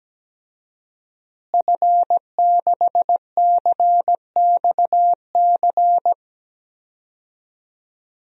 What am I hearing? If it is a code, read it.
Morse code, heard as a keyed tone, about 17 wpm, 700 Hz. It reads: F6CXC
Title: DXCC-100 Call Signs - 5 Characters - Repeated 2x - 17wpm